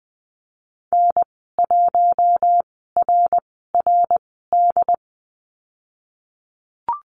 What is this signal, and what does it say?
Morse code, heard as a keyed tone, about 20 wpm, spaced as for 15 wpm, 700 Hz. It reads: N1RRD
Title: 076 Lesson 13g - Letter L - Call Signs 20wpm